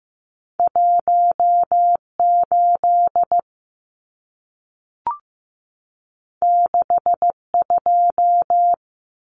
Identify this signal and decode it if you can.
Morse code, heard as a keyed tone, about 15 wpm, 700 Hz. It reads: 18 62
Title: Numbers - 2-Digit 15wpm